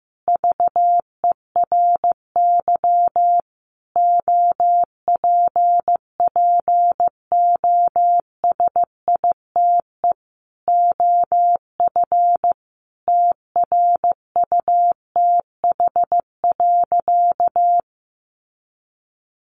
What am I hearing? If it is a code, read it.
Morse code, heard as a keyed tone, about 15 wpm, 700 Hz. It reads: VERY OPPOSITE OF TRUTH.
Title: Sentences from Top 1300 Words 15wpm